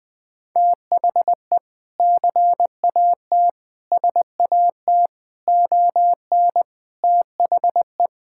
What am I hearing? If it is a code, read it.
Morse code, heard as a keyed tone, about 20 wpm, 700 Hz. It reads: THE CAT SAT ON THE
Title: Sentences with 2-Word Phrases - 20wpm